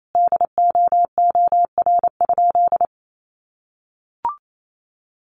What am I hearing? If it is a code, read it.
Morse code, heard as a keyed tone, about 28 wpm, 700 Hz. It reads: DOOR?
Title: Sentences from Top 1300 Words 28wpm